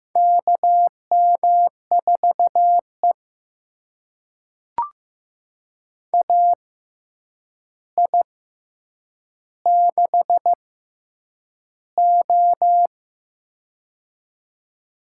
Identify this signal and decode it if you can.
Morse code, heard as a keyed tone, about 15 wpm, spaced as for 8 wpm, 700 Hz. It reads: KM4E AI6O
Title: USA Call Signs - 4 Characters - 6x Character Spacing - 15wpm